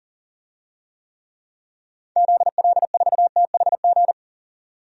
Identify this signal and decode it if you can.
Morse code, heard as a keyed tone, about 40 wpm, 700 Hz. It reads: ZL4THG
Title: DXCC-100 Call Signs - 6 Characters - 40wpm